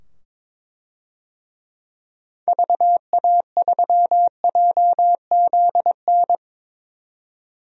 Morse code, heard as a keyed tone, about 22 wpm, 700 Hz. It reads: VA3JZN
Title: DXCC-100 Call Signs - 6 Characters - 22wpm